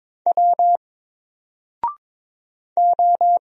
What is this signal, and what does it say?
Morse code, heard as a keyed tone, about 23 wpm, 700 Hz. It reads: W O
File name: Single-Letter-Number-22wpm